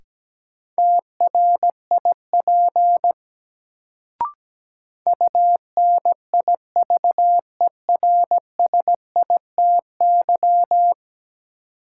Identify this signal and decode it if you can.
Morse code, heard as a keyed tone, about 17 wpm, 700 Hz. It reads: TRIP UNIVERSITY
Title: Sentences from Top 1000 Words - Review Part 4 - New Word Drill - 17wpm